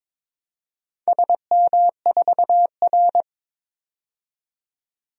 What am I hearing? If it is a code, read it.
Morse code, heard as a keyed tone, about 22 wpm, 700 Hz. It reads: SM4R
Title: DXCC-100 Call Signs - 4 Characters - 22wpm